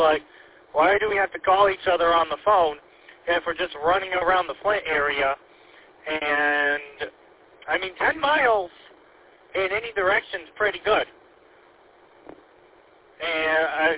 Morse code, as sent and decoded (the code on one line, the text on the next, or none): none